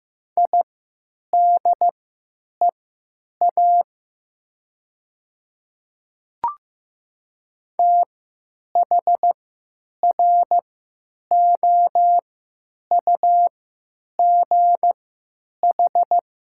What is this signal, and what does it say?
Morse code, heard as a keyed tone, about 15 wpm, spaced as for 9 wpm, 700 Hz. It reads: IDEA THROUGH
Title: 3x Character Spacing - Top 300 Words - 15wpm